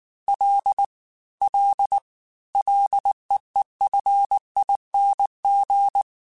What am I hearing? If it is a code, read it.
Morse code, heard as a keyed tone, about 19 wpm, 800 Hz. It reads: L L LEEFING
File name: Ls / Single L m3s / Single L Prefixe 3x2 Twice @18